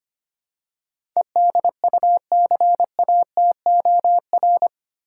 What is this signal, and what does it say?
Morse code, heard as a keyed tone, about 25 wpm, 700 Hz. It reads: EDUCATOR
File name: 8 Letter Words - First 3 Letters Given - 25wpm